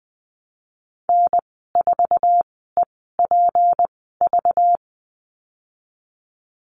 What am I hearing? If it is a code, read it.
Morse code, heard as a keyed tone, about 20 wpm, spaced as for 15 wpm, 700 Hz. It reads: N4EPV
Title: 207 Lesson 33g - Letter j - Call Signs 20wpm